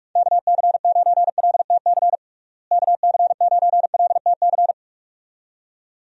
Morse code, heard as a keyed tone, about 41 wpm, 700 Hz. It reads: KC9LTC KC9LTC
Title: USA Call Signs - 6 Characters - Repeated 2x - 45wpm